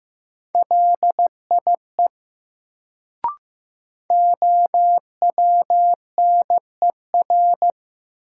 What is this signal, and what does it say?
Morse code, heard as a keyed tone, about 15 wpm, 700 Hz. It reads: LIE OWNER
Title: Sentences from Top 1600 Words - New Word Drill - 15wpm